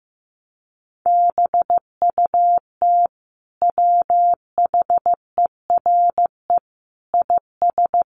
Morse code, heard as a keyed tone, about 15 wpm, 700 Hz. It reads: BUT WHERE IS
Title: Sentences from Top 200 Words 15wpm